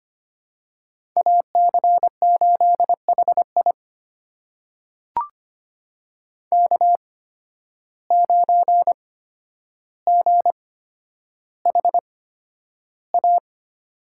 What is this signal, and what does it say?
Morse code, heard as a keyed tone, about 25 wpm, spaced as for 11 wpm, 700 Hz. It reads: AC8HI K9GHA
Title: USA Call Signs - 5 Characters - 8x Character Spacing - 25wpm